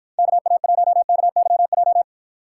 Morse code, heard as a keyed tone, about 53 wpm, 700 Hz. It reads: KA1KYJ